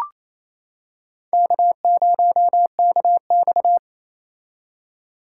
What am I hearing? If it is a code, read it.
Morse code, heard as a keyed tone, about 28 wpm, 700 Hz. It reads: K0KX
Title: K1USN SST QSO Practice - 28wpm